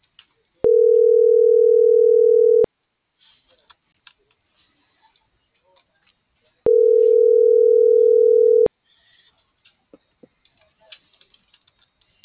Ringback tone (North America), 440 + 480 Hz.